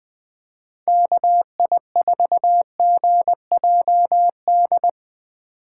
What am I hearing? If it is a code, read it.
Morse code, heard as a keyed tone, about 21 wpm, 700 Hz. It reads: KI4GJD
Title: USA Call Signs - 6 Characters - 2x Character Spacing - 20wpm